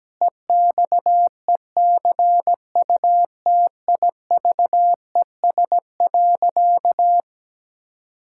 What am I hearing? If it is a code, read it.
Morse code, heard as a keyed tone, about 17 wpm, 700 Hz. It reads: EXECUTIVES.
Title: Pangram Challenge v2 - 17wpm